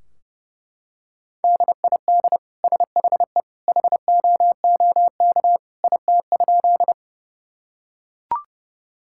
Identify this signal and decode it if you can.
Morse code, heard as a keyed tone, about 32 wpm, 700 Hz. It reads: DID SHE HOOK IT?
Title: Sentences from Top 1600 Words 30wpm